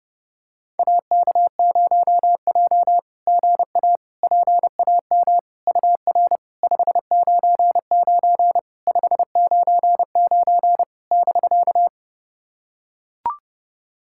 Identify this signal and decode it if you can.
Morse code, heard as a keyed tone, about 30 wpm, 690 Hz. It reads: AK0J GA PAM UR 599 599 <BK>
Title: SOTA QSO Practice 30wpm